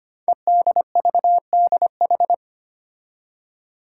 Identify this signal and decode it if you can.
Morse code, heard as a keyed tone, about 25 wpm, 700 Hz. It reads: EDVDH
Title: Code Groups - 5 Letters - 25wpm